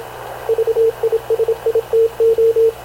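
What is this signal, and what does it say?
Morse code, heard as a keyed tone, about 26 wpm, 440 Hz, with radio noise. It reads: VISITO
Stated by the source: Havana Radio CLA press (PX) broadcast, recorded by Radio Officer Paul Dumesnil aboard mv CNAV QUEST/CZDO.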